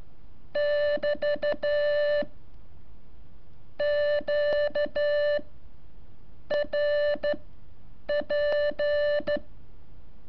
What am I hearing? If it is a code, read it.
Morse code, heard as a keyed tone, about 13 wpm, spaced as for 7 wpm, 580 Hz. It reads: <BT> QRP
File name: curso / ejercicio 18